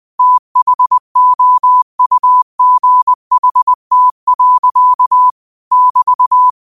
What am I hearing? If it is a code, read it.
Morse code, heard as a keyed tone, about 20 wpm, 1000 Hz. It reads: THOUGHT. <BT>
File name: 20wpm